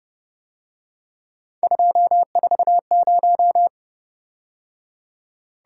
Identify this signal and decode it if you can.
Morse code, heard as a keyed tone, about 29 wpm, 700 Hz. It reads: 240